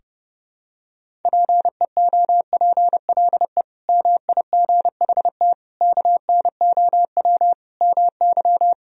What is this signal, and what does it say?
Morse code, heard as a keyed tone, about 30 wpm, 700 Hz. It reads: PEOPLE MIGHT KNOW MY
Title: Sentences from Top 500 Words and Abbreviations 30wpm